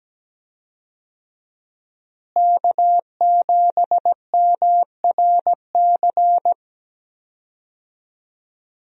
Morse code, heard as a keyed tone, about 17 wpm, 700 Hz. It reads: K7MRC
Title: USA Call Signs - 5 Characters - Repeated 2x - 17wpm